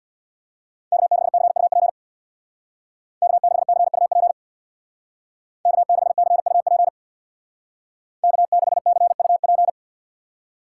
Morse code, heard as a keyed tone, about 59 wpm, 700 Hz. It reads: K6CUP K6CUP K6CUP K6CUP
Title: Speed Racing - USA 1x3 Call Signs 50wpm